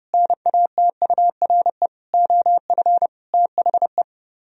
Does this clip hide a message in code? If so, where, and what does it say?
Yes: Morse code from 0.1 to 4.0 s, heard as a keyed tone, about 30 wpm, 700 Hz. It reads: NATURE OF THE